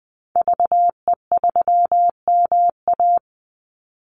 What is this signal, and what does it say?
Morse code, heard as a keyed tone, about 20 wpm, 700 Hz. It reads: VE3MA
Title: Call Signs - 20wpm